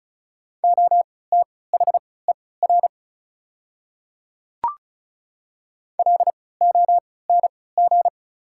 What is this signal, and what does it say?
Morse code, heard as a keyed tone, about 36 wpm, spaced as for 20 wpm, 700 Hz. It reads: OTHER LONG
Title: 3x Character Spacing - Top 100 Words - 35wpm